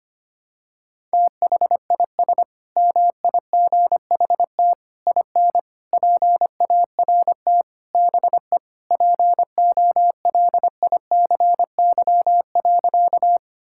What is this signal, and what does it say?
Morse code, heard as a keyed tone, about 25 wpm, 700 Hz. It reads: THIS MIGHT IN PART BE POLICY.
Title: Sentences from Top 1000 Words - Review Part 5 - 25wpm